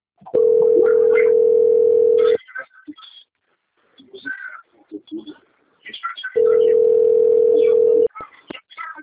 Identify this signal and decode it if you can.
Ringback tone (North America), 440 + 480 Hz, with speech.
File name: out- - - - - .54054